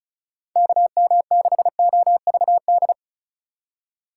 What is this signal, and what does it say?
Morse code, heard as a keyed tone, about 35 wpm, 690 Hz. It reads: KM6OVD
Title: USA Call Signs - 6 Characters - 3x Character Spacing - 35wpm